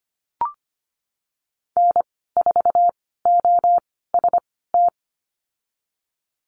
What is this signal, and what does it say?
Morse code, heard as a keyed tone, about 25 wpm, spaced as for 16 wpm, 700 Hz. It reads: N4OST
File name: Lesson 15g - Number 5 - Call Signs 25wpm